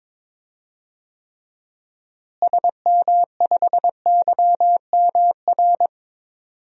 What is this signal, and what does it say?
Morse code, heard as a keyed tone, about 22 wpm, 700 Hz. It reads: SM5YMR
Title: DXCC-100 Call Signs - 6 Characters - Repeated 3x - 22wpm